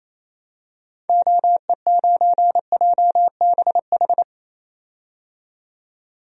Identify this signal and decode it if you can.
Morse code, heard as a keyed tone, about 28 wpm, 700 Hz. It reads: OE9JBH
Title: DXCC-100 Call Signs - 6 Characters - 28wpm